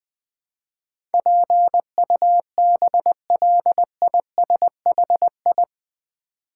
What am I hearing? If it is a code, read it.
Morse code, heard as a keyed tone, about 20 wpm, 700 Hz. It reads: PUBLISHI
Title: Build A Word - 4 Letters At A Time - 10-Letter Words - 20wpm